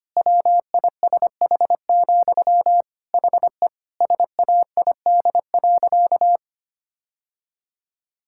Morse code, heard as a keyed tone, about 25 wpm, 700 Hz. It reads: WISH, HE SAID.
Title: Sentences from Top 1600 Words 25wpm